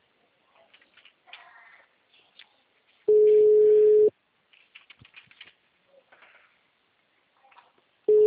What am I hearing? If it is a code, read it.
Ringback tone (Europe, 425 Hz).